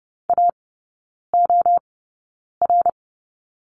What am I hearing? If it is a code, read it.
Morse code, heard as a keyed tone, about 30 wpm, spaced as for 9 wpm, 700 Hz. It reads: AOR
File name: Lesson 16e - Letter c - Three Characters 30wpm